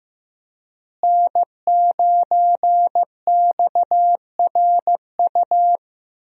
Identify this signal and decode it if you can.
Morse code, heard as a keyed tone, about 15 wpm, 700 Hz. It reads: N9XRU
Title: USA Call Signs - 5 Characters - Repeated 3x - 15wpm